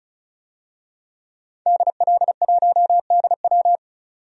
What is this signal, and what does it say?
Morse code, heard as a keyed tone, about 34 wpm, 700 Hz. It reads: DL1DW
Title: DXCC-100 Call Signs - 5 Characters - 35wpm